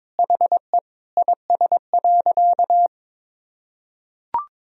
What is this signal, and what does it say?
Morse code, heard as a keyed tone, about 22 wpm, 700 Hz. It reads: HE IS.
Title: Sentences from Top 1000 Words - Review Part 1 - 22wpm